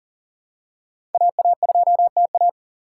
Morse code, heard as a keyed tone, about 38 wpm, 700 Hz. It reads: AA2TA
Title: USA Call Signs - 5 Characters - 3x Character Spacing - 40wpm